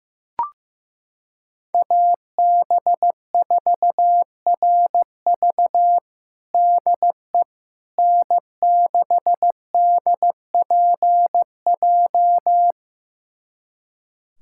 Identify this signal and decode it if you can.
Morse code, heard as a keyed tone, about 15 wpm, 700 Hz. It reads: AB4RV DE N6DPJ
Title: USA Callsign DE Callsign - 5 Characters - 15wpm